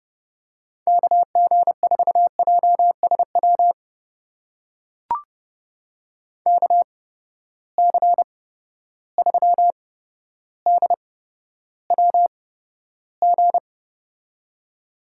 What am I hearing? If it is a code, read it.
Morse code, heard as a keyed tone, about 30 wpm, spaced as for 13 wpm, 700 Hz. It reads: KG4JSW KC3DWG
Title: USA Call Signs - 6 Characters - 8x Character Spacing - 30wpm